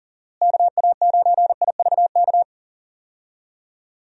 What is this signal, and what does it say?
Morse code, heard as a keyed tone, about 40 wpm, 700 Hz. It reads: KA9IVK
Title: USA Call Signs - 6 Characters - Repeated 2x - 40wpm